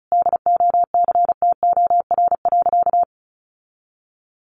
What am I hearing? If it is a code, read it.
Morse code, heard as a keyed tone, about 34 wpm, 700 Hz. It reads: DOCTOR.
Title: Sentences from Top 1500 Words 35wpm